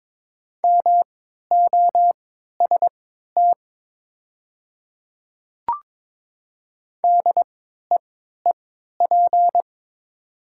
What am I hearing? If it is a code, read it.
Morse code, heard as a keyed tone, about 22 wpm, spaced as for 12 wpm, 700 Hz. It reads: MOST DEEP
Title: 3x Character Spacing - Top 500 Words - 22wpm